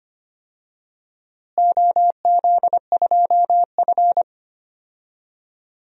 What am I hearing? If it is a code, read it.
Morse code, heard as a keyed tone, about 25 wpm, 700 Hz. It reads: OZ2F
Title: DXCC-100 Call Signs - 4 Characters - Repeated 3x - 25wpm